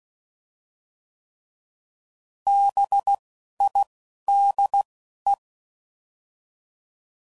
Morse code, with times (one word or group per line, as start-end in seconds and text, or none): 2.5-3.1 B
3.6-3.8 I
4.3-4.8 D
5.3-5.3 E